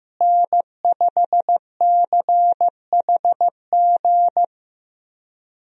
Morse code, heard as a keyed tone, about 15 wpm, 700 Hz. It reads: N5CHG
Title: USA Call Signs - 5 Characters - Repeated 2x - 15wpm